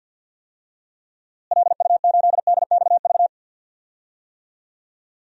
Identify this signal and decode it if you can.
Morse code, heard as a keyed tone, about 53 wpm, 700 Hz. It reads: LU8DXV
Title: DXCC-100 Call Signs - 6 Characters - Repeated 2x - 50wpm